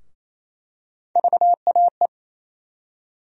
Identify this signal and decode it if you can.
Morse code, heard as a keyed tone, about 28 wpm, 700 Hz. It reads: VAE
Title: USA Call Sign Postfixes - 3 Letters 28wpm